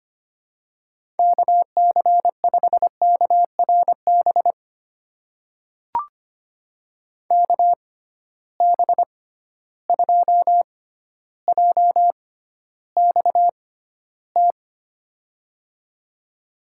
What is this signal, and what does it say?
Morse code, heard as a keyed tone, about 25 wpm, spaced as for 13 wpm, 700 Hz. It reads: KC5KRB KB2JXT
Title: USA Call Signs - 6 Characters - 6x Character Spacing - 25wpm